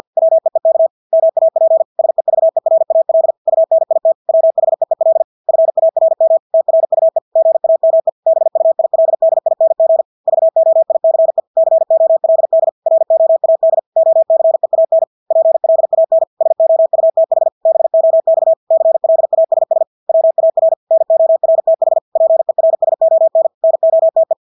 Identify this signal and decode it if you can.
Morse code, heard as a keyed tone, about 50 wpm, 650 Hz. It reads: WEEK MAP SEVERAL UNIT WHEEL FARM TRUE GAME BUILDING VOICE COLD ROAD OCEAN PLAN SOUTH BOX CLASS WAR NORTH PERSON NOTE
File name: Engelske ord 50 wpm